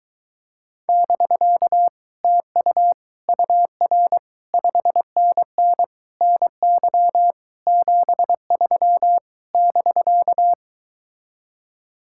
Morse code, heard as a keyed tone, about 23 wpm, 700 Hz. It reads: <BK> TU UR 5NN NY 73 <BK>